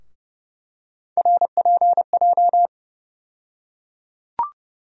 Morse code, heard as a keyed tone, about 30 wpm, 700 Hz. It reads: RPJ
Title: USA Call Sign Postfixes - 3 Letters 30wpm